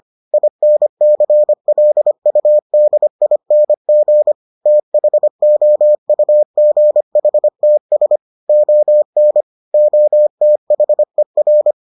Morse code, heard as a keyed tone, about 25 wpm, 600 Hz. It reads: INCLUDING THOUGHTS ON OTHER